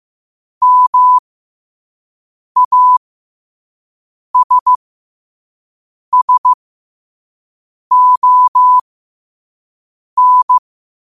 Morse code, heard as a keyed tone, about 15 wpm, spaced as for 5 wpm, 1000 Hz. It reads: MASSON
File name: qotdfiles / 5wpm